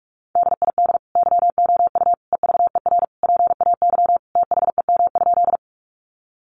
Morse code, heard as a keyed tone, about 44 wpm, 700 Hz. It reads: DID YOU EVER PAY THEM?
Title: Sentences from Top 700 Words 45wpm